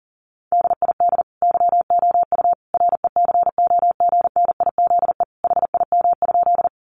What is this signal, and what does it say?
Morse code, heard as a keyed tone, about 40 wpm, 700 Hz. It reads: DID YOU RECOGNIZE HIM?
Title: Sentences from Top 1400 Words 40wpm